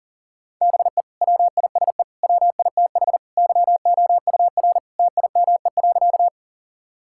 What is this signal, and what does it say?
Morse code, heard as a keyed tone, about 40 wpm, 700 Hz. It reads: BE WISE WITH YOUR TIME.
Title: Sentences from Top 1600 Words 40wpm